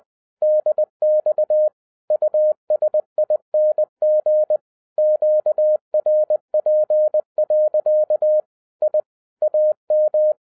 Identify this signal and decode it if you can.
Morse code, heard as a keyed tone, about 20 wpm, 610 Hz. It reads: DX USING QRP. I AM